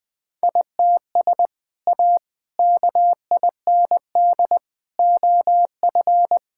Morse code, heard as a keyed tone, about 21 wpm, 700 Hz. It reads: ITS A KIND OF
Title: Sentences with 3-Word Phrases - 20wpm